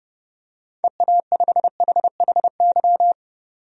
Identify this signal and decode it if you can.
Morse code, heard as a keyed tone, about 30 wpm, 700 Hz. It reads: EA5HHY
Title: DXCC-100 Call Signs - 6 Characters - 30wpm